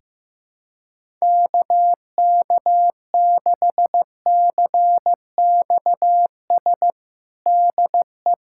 Morse code, heard as a keyed tone, about 15 wpm, 700 Hz. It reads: KK6CXS DE
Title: USA Callsign DE Callsign - 6 Characters - 15wpm